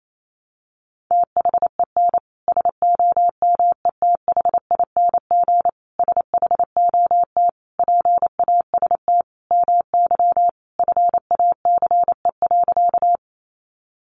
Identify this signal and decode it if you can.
Morse code, heard as a keyed tone, about 28 wpm, 700 Hz. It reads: THEN SOMETHING SHOT PAST MY FACE.
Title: Sentences from Top 1100 Words 28wpm